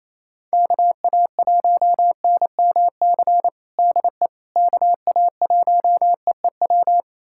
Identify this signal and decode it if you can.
Morse code, heard as a keyed tone, about 27 wpm, 700 Hz. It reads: KA1NMC DE KA1EEW
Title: USA Callsign DE Callsign 28wpm